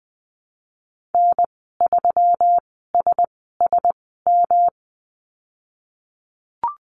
Morse code, heard as a keyed tone, about 19 wpm, spaced as for 14 wpm, 700 Hz. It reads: N3SSM
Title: 128 Lesson 21g - Number 6 - Call Signs 20wpm